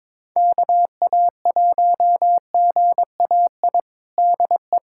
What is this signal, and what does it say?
Morse code, heard as a keyed tone, about 22 wpm, 700 Hz. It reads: KA1GAI DE
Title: USA Callsign DE Callsign - 6 Characters - 22wpm